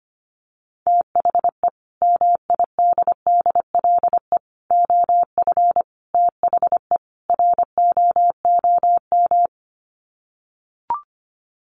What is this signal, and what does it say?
Morse code, heard as a keyed tone, about 25 wpm, 700 Hz. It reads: THE MIDDLE OF THE ROOM
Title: Sets of 5 Words Encore2 25wpm